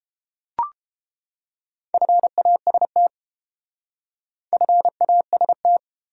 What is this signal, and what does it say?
Morse code, heard as a keyed tone, about 32 wpm, 700 Hz. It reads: FAST FAST